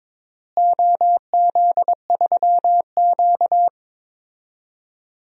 Morse code, heard as a keyed tone, about 22 wpm, 700 Hz. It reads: OZ3Q
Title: DXCC-100 Call Signs - 4 Characters - Repeated 3x - 22wpm